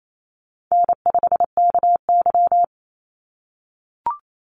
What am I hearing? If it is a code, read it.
Morse code, heard as a keyed tone, about 28 wpm, 700 Hz. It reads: N5KY
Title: Call Signs - 28wpm